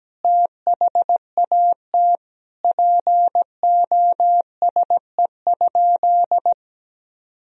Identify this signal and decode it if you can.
Morse code, heard as a keyed tone, about 17 wpm, 700 Hz. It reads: THAT POSE?
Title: Sentences from Top 1000 Words - Review Part 2 - 17wpm